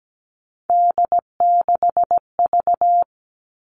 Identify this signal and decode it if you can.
Morse code, heard as a keyed tone, about 17 wpm, 700 Hz. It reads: D6V
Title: Three Challenging Letter-Number Combinations 17wpm